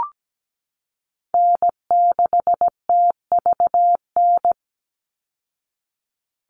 Morse code, heard as a keyed tone, about 17 wpm, 700 Hz. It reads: N6TVN
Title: Call Signs Repeated 2x - 17wpm